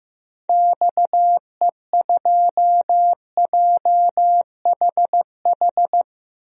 Morse code, heard as a keyed tone, about 15 wpm, 700 Hz. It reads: XE2JHH
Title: DXCC-100 Call Signs - 6 Characters - 15wpm